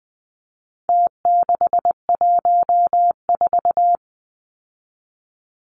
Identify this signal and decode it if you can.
Morse code, heard as a keyed tone, about 20 wpm, 700 Hz. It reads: T614
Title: Numbers - Contest Style - 4-Digit 20wpm